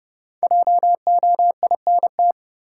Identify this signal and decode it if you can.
Morse code, heard as a keyed tone, about 30 wpm, 700 Hz. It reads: JOINT
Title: Sentences from Top 1300 Words - New Word Drill Alt - 30wpm